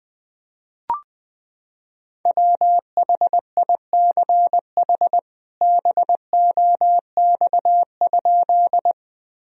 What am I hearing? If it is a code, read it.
Morse code, heard as a keyed tone, about 20 wpm, 700 Hz. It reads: WHICH BOX?
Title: Sentences from Top 1000 Words - Review Part 2 - 20wpm